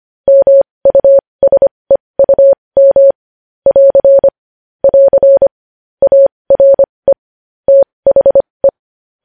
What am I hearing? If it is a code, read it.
Morse code, heard as a keyed tone, about 25 wpm, spaced as for 20 wpm, 550 Hz. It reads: MUSEUM <AR> <AR> ARE THE
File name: podcast / News headlines-1x-25x20